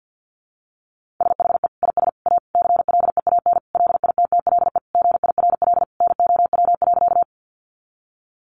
Morse code, heard as a keyed tone, about 50 wpm, 700 Hz. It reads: SHE IS A CLEAN LITTLE GIRL NOW.
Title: Sentences from Top 1100 Words 50wpm